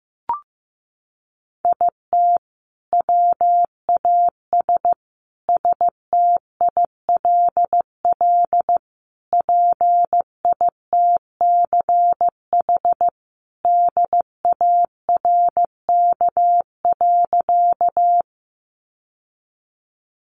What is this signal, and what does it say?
Morse code, heard as a keyed tone, about 15 wpm, 700 Hz. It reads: IT WAS STILL PITCH DARK.
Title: Sentences from Top 1300 Words 15wpm